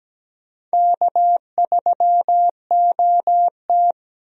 Morse code, heard as a keyed tone, about 17 wpm, 700 Hz. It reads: K3OT
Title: USA Call Signs - 4 Characters - 17wpm